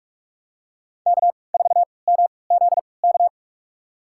Morse code, heard as a keyed tone, about 46 wpm, spaced as for 26 wpm, 700 Hz. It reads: K4MZK
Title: USA Call Signs - 5 Characters - 3x Character Spacing - 45wpm